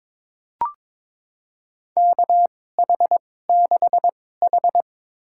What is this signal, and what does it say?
Morse code, heard as a keyed tone, about 21 wpm, spaced as for 16 wpm, 700 Hz. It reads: KH6H
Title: USA Call Signs - 4 Characters - 2x Character Spacing - 22wpm